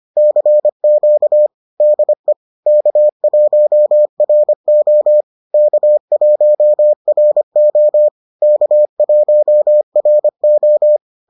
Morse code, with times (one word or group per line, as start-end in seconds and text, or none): 0.2-1.5 CQ
1.8-2.3 DE
2.7-5.2 K1RO
5.5-8.1 K1RO
8.4-11.0 K1RO